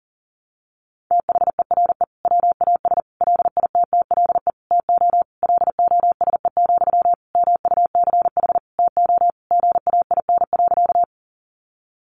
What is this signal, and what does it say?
Morse code, heard as a keyed tone, about 40 wpm, 700 Hz. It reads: THERE WAS LITTLE TO LOSE, MUCH TO GAIN.
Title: Sentences from Top 1100 Words 40wpm